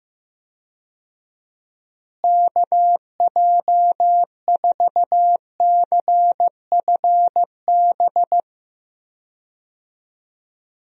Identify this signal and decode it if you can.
Morse code, heard as a keyed tone, about 15 wpm, 700 Hz. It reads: KJ4CFB
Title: USA Call Signs - 6 Characters - Repeated 2x - 15wpm